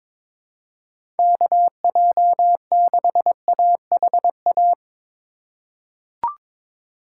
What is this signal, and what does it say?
Morse code, heard as a keyed tone, about 22 wpm, 700 Hz. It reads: KJ6AHA
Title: USA Call Signs - 6 Characters - 4x Character Spacing - 22wpm